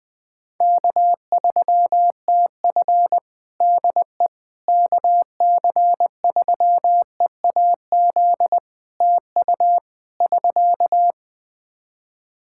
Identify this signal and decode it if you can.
Morse code, heard as a keyed tone, about 20 wpm, 700 Hz. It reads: K3TF DE KC3EAZ TU <SK>
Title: Practice QSO - Exchange 4 - Ending - 20wpm